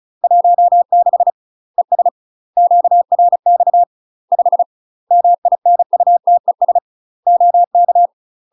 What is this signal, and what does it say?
Morse code, heard as a keyed tone, about 35 wpm, 700 Hz. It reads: 16 ES QRX 5 MINUTES OK
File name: UK MASTER QSOs TRAINING / 35WPM / UK MASTER QSOs 0022